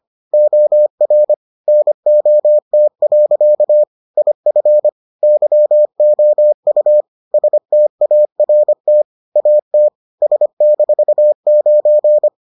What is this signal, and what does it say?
Morse code, heard as a keyed tone, about 25 wpm, 600 Hz. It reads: OR NOT. IF YOU START AT S-9